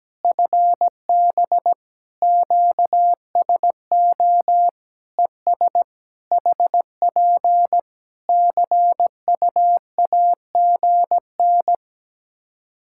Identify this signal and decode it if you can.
Morse code, heard as a keyed tone, about 17 wpm, 700 Hz. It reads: FB QSO ES HP CUAGN
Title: Practice QSO - Exchange 4 - Ending - 17wpm